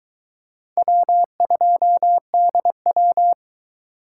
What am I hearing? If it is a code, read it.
Morse code, heard as a keyed tone, about 23 wpm, 700 Hz. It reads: W2DW